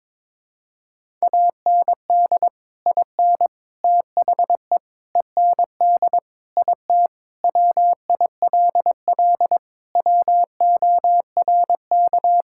Morse code, heard as a keyed tone, about 22 wpm, 700 Hz. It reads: AND IN THE END IT WILL WORK
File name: practice / 22wpm / Sentences with 3-Word Phrases - 22wpm